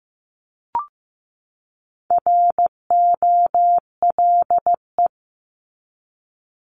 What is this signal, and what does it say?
Morse code, heard as a keyed tone, about 15 wpm, 700 Hz. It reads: ROLE
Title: Top 751-1000 Words 15wpm